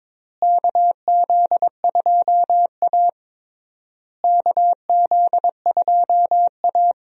Morse code, heard as a keyed tone, about 22 wpm, 700 Hz. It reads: KZ2A KZ2A
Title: USA Call Signs - 4 Characters - Repeated 3x - 22wpm